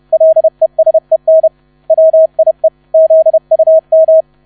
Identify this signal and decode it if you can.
Morse code, heard as a keyed tone, about 31 wpm, 640 Hz. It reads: LESEN WIE ZUM
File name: 30wpm